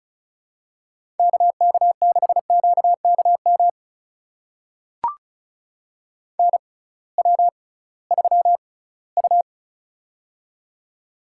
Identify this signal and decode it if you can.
Morse code, heard as a keyed tone, about 34 wpm, spaced as for 20 wpm, 700 Hz. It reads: KK6QKM NW3U
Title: USA Call Signs - 6x Character Spacing - 35wpm